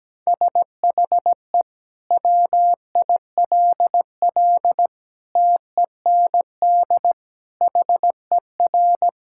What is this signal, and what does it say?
Morse code, heard as a keyed tone, about 17 wpm, 700 Hz. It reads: SHE WILL TEND HER
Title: Sentences from Top 1000 Words - Review Part 5 - 17wpm